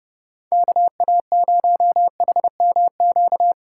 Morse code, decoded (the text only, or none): KA0HMQ